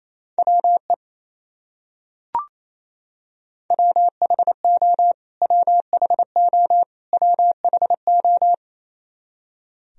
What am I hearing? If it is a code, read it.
Morse code, heard as a keyed tone, about 28 wpm, 700 Hz. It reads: WE WHO WHO WHO
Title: Top 100 Words Repeated 28wpm